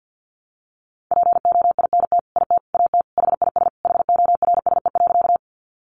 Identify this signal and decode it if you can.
Morse code, heard as a keyed tone, about 49 wpm, 700 Hz. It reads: POINT IT AT HIS HORSE.
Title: Sentences from Top 300 Words 50wpm